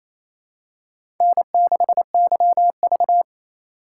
Morse code, heard as a keyed tone, about 28 wpm, 700 Hz. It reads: N6YV
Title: USA Call Signs - 4 Characters - 2x Character Spacing - 28wpm